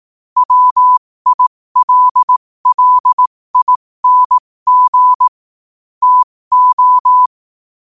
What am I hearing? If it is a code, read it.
Morse code, heard as a keyed tone, about 18 wpm, spaced as for 15 wpm, 1000 Hz. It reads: WILLING TO
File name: qotdfiles / 15wpm